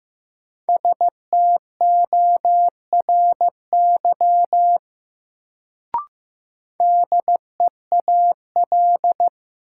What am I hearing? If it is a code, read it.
Morse code, heard as a keyed tone, about 15 wpm, 700 Hz. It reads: STORY DEAL
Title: Sets of 3 Words - New Word Drill - 15wpm